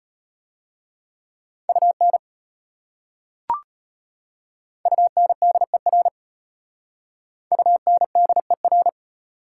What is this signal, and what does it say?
Morse code, heard as a keyed tone, about 33 wpm, 700 Hz. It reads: UN UNDER UNDER